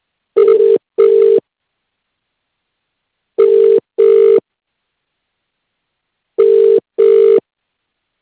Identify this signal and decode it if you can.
Ringback tone (UK), 400 + 450 Hz.